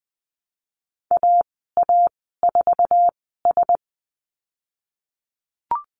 Morse code, heard as a keyed tone, about 20 wpm, spaced as for 15 wpm, 700 Hz. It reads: AA4S
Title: 055 Lesson 10g - Letter r - Call Signs 20wpm